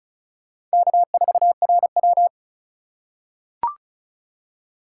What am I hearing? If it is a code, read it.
Morse code, heard as a keyed tone, about 36 wpm, 700 Hz. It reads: K4RW
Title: USA Call Signs - 2x Character Spacing - 35wpm